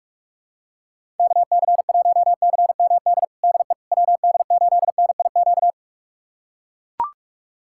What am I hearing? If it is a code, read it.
Morse code, heard as a keyed tone, about 42 wpm, 700 Hz. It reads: KC1CMD DE WD8NIQ
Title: USA Callsign DE Callsign 45wpm